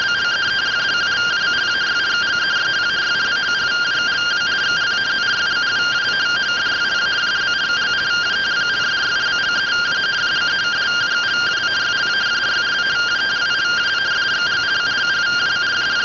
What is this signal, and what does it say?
Two-tone teleprinter signal, about 170 Hz shift.